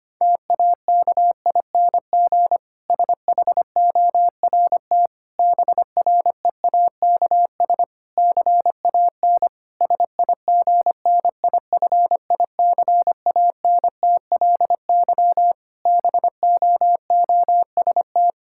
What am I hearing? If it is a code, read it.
Morse code, heard as a keyed tone, about 25 wpm, 700 Hz. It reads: TAKING SHORT BREAKS CAN SIGNIFICANTLY BOOST
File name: practice / 25wpm / Difficult Sentences with Hints - 25wpm